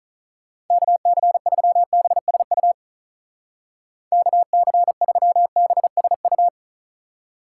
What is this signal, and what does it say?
Morse code, heard as a keyed tone, about 45 wpm, 700 Hz. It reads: KC3BSU KC3BSU